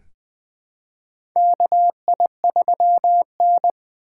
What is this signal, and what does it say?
Morse code, heard as a keyed tone, about 20 wpm, 700 Hz. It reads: KI3N
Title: USA Call Signs - 4 Characters - 3x Character Spacing - 20wpm